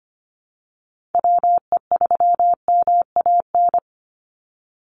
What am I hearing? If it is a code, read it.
Morse code, heard as a keyed tone, about 25 wpm, 700 Hz. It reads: WE3MAN
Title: USA Call Signs 25wpm